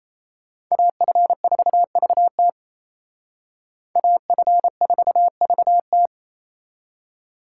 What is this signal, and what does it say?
Morse code, heard as a keyed tone, about 32 wpm, 700 Hz. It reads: AF4VT AF4VT
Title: Speed Racing - USA 2x2 Call Signs 28wpm